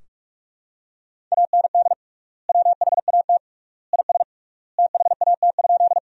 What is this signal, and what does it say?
Morse code, heard as a keyed tone, about 42 wpm, 700 Hz. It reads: AND WHAT IS THAT?
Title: 3x Word-Spacing - Sentences from Top 100 Words - 45wpm